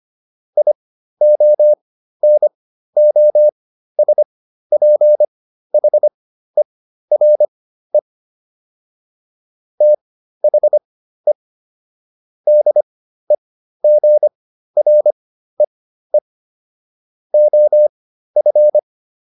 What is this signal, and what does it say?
Morse code, heard as a keyed tone, about 25 wpm, spaced as for 13 wpm, 600 Hz. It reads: IONOSPHERE THE DEGREE OF